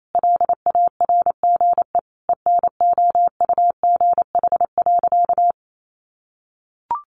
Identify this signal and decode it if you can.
Morse code, heard as a keyed tone, about 28 wpm, 700 Hz. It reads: LARGE ENOUGH.
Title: Sentences from Top 1500 Words 28wpm